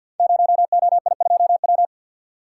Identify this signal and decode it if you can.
Morse code, heard as a keyed tone, about 51 wpm, 700 Hz. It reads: 0OI2W
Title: Code Groups - 5 Letter:Numbers - 50wpm